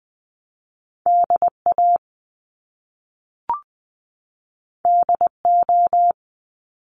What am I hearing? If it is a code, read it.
Morse code, heard as a keyed tone, about 20 wpm, 700 Hz. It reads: DA DO
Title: Top Two Letter Combinations in Words 20wpm